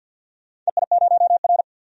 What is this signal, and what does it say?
Morse code, heard as a keyed tone, about 46 wpm, 700 Hz. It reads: EI0R
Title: DXCC-100 Call Signs - 4 Characters - 50wpm